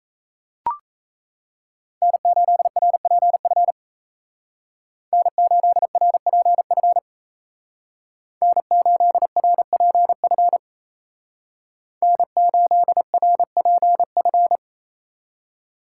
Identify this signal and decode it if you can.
Morse code, heard as a keyed tone, about 34 wpm, 700 Hz. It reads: N8RPF N8RPF N8RPF N8RPF
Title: Speed Racing - USA 1x3 Call Signs 28wpm